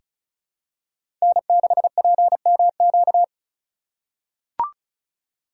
Morse code, heard as a keyed tone, about 36 wpm, 700 Hz. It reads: N6PMQ
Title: USA Call Signs - 5 Characters - 2x Character Spacing - 35wpm